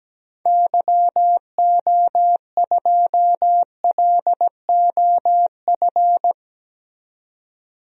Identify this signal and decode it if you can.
Morse code, heard as a keyed tone, about 17 wpm, 700 Hz. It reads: YO2LOF
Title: DXCC-100 Call Signs - 6 Characters - 17wpm